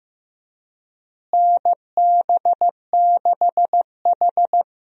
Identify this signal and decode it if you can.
Morse code, heard as a keyed tone, about 15 wpm, 700 Hz. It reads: NB6H
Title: USA Call Signs - 4 Characters - Repeated 3x - 15wpm